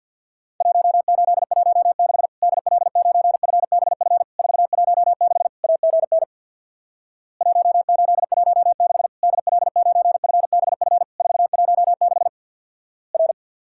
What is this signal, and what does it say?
Morse code, heard as a keyed tone, about 50 wpm, 700 Hz. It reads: 1816 DL9FBF 416 AGN 1816 DL9FBF 416 R
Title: Worked All Europe DX Contest - QTC Exchange with Repeats - 50wpm